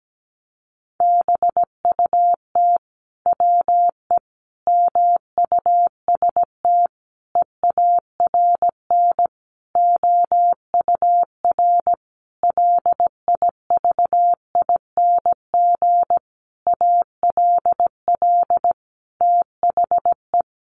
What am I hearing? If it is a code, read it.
Morse code, heard as a keyed tone, about 17 wpm, 700 Hz. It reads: BUT WE MUST EARN OUR LIVING ALL THE
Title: Sentences from Top 1500 Words 17wpm